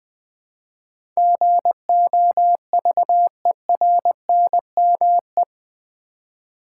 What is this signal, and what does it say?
Morse code, heard as a keyed tone, about 20 wpm, 700 Hz. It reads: GOVERNME
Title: Build A Word - 4 Letters At A Time - 10-Letter Words - 20wpm